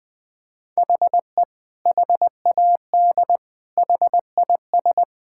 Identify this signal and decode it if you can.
Morse code, heard as a keyed tone, about 20 wpm, 700 Hz. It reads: HE HAD HIS
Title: Sentences from Top 1000 Words - Review Part 5 - 20wpm